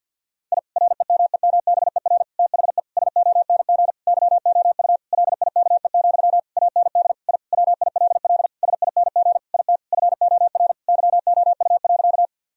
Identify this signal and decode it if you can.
Morse code, heard as a keyed tone, about 50 wpm, 700 Hz. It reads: I REMEMBER THE SONG YOU LIKE, AND I WILL SING IT FOR YOU.
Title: Sentences from Top 1000 Words - Review Part 2 - 50wpm